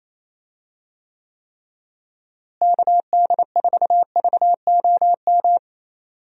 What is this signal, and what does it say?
Morse code, heard as a keyed tone, about 27 wpm, 700 Hz. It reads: KD4VOM
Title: USA Call Signs - 6 Characters - Repeated 3x - 28wpm